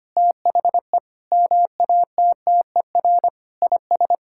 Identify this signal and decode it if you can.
Morse code, heard as a keyed tone, about 25 wpm, 700 Hz. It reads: THE MATTER IS
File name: Sets of 9 Words - 25wpm